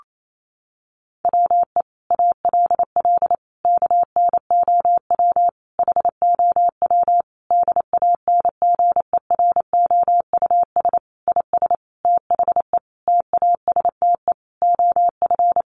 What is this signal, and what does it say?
Morse code, heard as a keyed tone, about 28 wpm, 700 Hz. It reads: WE ALL KNOW HOW DANGEROUS IS THE TASTE OF
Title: Sentences from Top 1200 Words 28wpm